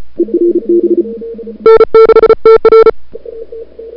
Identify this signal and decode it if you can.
Morse code, heard as a keyed tone, about 35 wpm, 440 Hz. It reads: N6TR
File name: n6tr / Fab2007CWSprintHour1